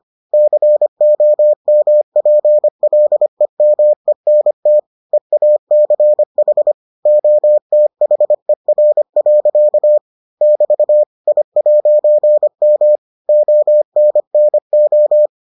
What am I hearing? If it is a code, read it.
Morse code, heard as a keyed tone, about 25 wpm, 600 Hz. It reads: COMPLEMENT EACH OTHER. <BT> I'M ONNO